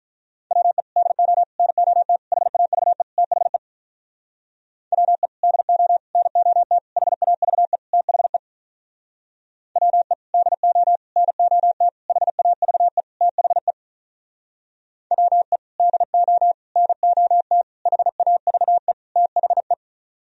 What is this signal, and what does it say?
Morse code, heard as a keyed tone, about 46 wpm, 700 Hz. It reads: WE DO NOT HAVE THE WE DO NOT HAVE THE WE DO NOT HAVE THE WE DO NOT HAVE THE